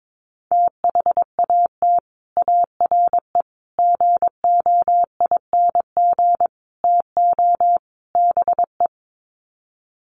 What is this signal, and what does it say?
Morse code, heard as a keyed tone, about 22 wpm, 700 Hz. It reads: THAT ARE GOING TO BE
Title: Sets of 5 Words Encore2 22wpm